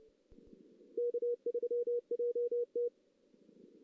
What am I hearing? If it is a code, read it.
Morse code, heard as a keyed tone, about 30 wpm, 440 Hz. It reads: K3JT
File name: 2016 JAN CW NAQP / 2016 Jan naqp k5ot - 0-3